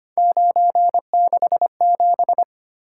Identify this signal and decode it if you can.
Morse code, heard as a keyed tone, about 25 wpm, 700 Hz. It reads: 967
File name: Three Letter-Number Combinations - 25wpm